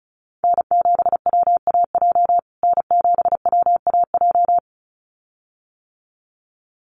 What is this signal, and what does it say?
Morse code, heard as a keyed tone, about 34 wpm, 700 Hz. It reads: N7WAJ N7WAJ
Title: USA Call Signs Repeated 2x 35wpm